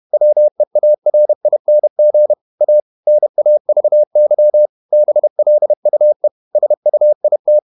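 Morse code, heard as a keyed tone, about 31 wpm, 600 Hz. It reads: WEARING A NAVY BLUE SUIT
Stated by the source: Morse code at 30 wpm, the novel Never by Ken Follett in English, from CW med Gnister.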